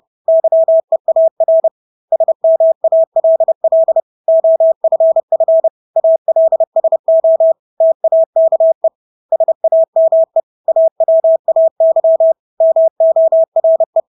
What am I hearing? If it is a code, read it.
Morse code, heard as a keyed tone, about 30 wpm, 650 Hz. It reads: YEAR SMALL OFF ALSO TAKE SAME AWAY MORE
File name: Engelske ord 30-30 wpm